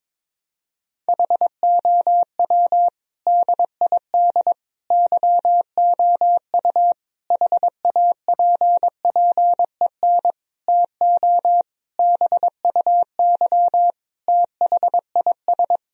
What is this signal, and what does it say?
Morse code, heard as a keyed tone, about 22 wpm, 700 Hz. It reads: HOW DID YOU HAPPEN TO BUY THIS
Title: Sentences from Top 1000 Words - Review Part 4 - 22wpm